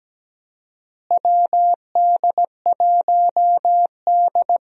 Morse code, heard as a keyed tone, about 17 wpm, 700 Hz. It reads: WD1D
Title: USA Call Signs - 4 Characters - Repeated 2x - 17wpm